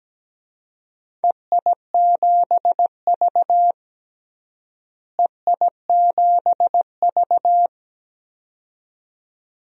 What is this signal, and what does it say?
Morse code, heard as a keyed tone, about 17 wpm, 700 Hz. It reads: EI7V EI7V
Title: DXCC-100 Call Signs - 4 Characters - Repeated 2x - 17wpm